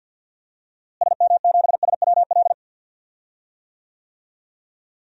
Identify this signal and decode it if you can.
Morse code, heard as a keyed tone, about 52 wpm, 700 Hz. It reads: SM7SWL
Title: DXCC-100 Call Signs - 6 Characters - Repeated 2x - 50wpm